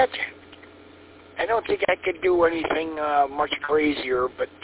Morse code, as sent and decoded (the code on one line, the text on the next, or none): none